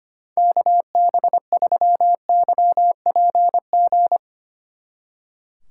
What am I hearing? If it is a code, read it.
Morse code, heard as a keyed tone, about 32 wpm, 700 Hz. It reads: KB3YPG